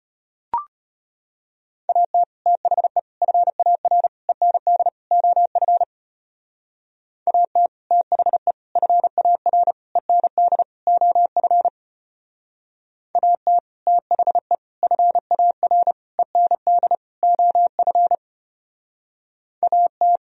Morse code, heard as a keyed tone, about 32 wpm, 700 Hz. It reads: AT THE FAR END OF AT THE FAR END OF AT THE FAR END OF AT